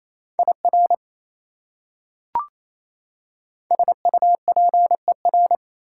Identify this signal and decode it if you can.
Morse code, heard as a keyed tone, about 28 wpm, 700 Hz. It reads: IR SUPER
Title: Prefixes and Suffixes 28wpm